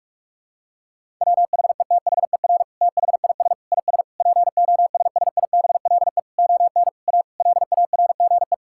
Morse code, heard as a keyed tone, about 46 wpm, 700 Hz. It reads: WHETHER THIS IS POSSIBLE ON A LARGE
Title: Sentences from Top 1000 Words - Review Part 5 - 45wpm